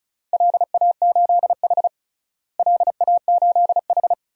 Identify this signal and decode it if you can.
Morse code, heard as a keyed tone, about 36 wpm, 700 Hz. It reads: LA8H LA8H
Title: DXCC-100 Call Signs - 4 Characters - Repeated 2x - 35wpm